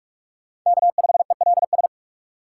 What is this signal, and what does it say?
Morse code, heard as a keyed tone, about 46 wpm, 700 Hz. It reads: K5ELS
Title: USA Call Signs - 5 Characters - 4x Character Spacing - 45wpm